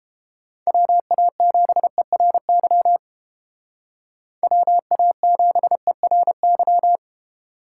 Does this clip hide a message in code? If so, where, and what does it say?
Yes: Morse code from 0.7 to 7.0 s, heard as a keyed tone, about 29 wpm, 700 Hz. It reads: WA7ERY WA7ERY